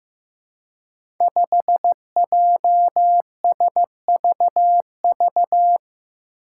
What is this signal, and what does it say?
Morse code, heard as a keyed tone, about 15 wpm, 700 Hz. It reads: 5JSVV
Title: Five Challenging Letter-Number Combinations 15wpm